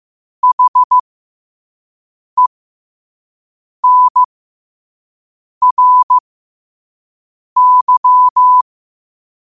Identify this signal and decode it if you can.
Morse code, heard as a keyed tone, about 15 wpm, spaced as for 5 wpm, 1000 Hz. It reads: HENRY
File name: qotdfiles / 5wpm